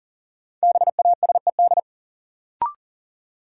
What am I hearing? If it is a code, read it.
Morse code, heard as a keyed tone, about 40 wpm, 700 Hz. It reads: BASED
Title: Sets of 2 Words - New Word Drill - 40wpm